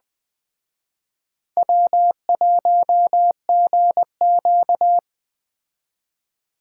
Morse code, heard as a keyed tone, about 20 wpm, 700 Hz. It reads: W1GQ
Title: Speed Racing - USA 1x2 Call Signs 17wpm